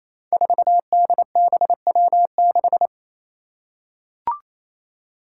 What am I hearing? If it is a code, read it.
Morse code, heard as a keyed tone, about 27 wpm, 700 Hz. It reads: 4DBW6
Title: Five Challenging Letter-Number Combinations 28wpm